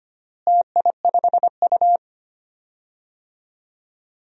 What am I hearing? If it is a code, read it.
Morse code, heard as a keyed tone, about 25 wpm, 700 Hz. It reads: TI5U
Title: DXCC-100 Call Signs - 4 Characters - Repeated 3x - 25wpm